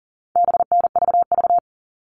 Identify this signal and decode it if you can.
Morse code, heard as a keyed tone, about 40 wpm, 700 Hz. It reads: BNVV
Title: All Cut Numbers - 4-Digit 40wpm